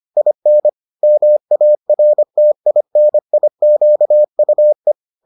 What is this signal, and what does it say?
Morse code, heard as a keyed tone, about 25 wpm, 600 Hz. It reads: IN MARTINIQUE
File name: 25wpm-6